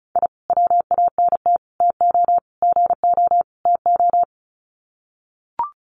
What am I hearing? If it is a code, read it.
Morse code, heard as a keyed tone, about 34 wpm, 700 Hz. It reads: I WANT TO GO TO
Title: Sets of 5 Words Encore 35wpm